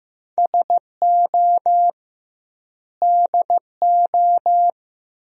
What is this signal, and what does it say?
Morse code, heard as a keyed tone, about 15 wpm, 700 Hz. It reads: SO DO
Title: 2x Word-Spacing - Sentences from Top 400 Words - 15wpm